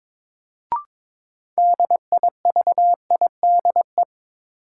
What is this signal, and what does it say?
Morse code, heard as a keyed tone, about 22 wpm, 700 Hz. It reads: DIVIDE
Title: Sentences from Top 1600 Words - New Word Drill - 22wpm